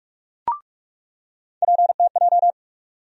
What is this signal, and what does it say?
Morse code, heard as a keyed tone, about 44 wpm, 700 Hz. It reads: PTJ
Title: USA Call Sign Postfixes - 3 Letters 45wpm